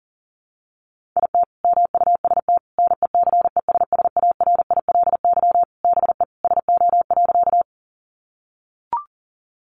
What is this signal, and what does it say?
Morse code, heard as a keyed tone, about 40 wpm, 700 Hz. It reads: IT MUST NECESSARILY BE SO.
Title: Sentences from Top 1400 Words 40wpm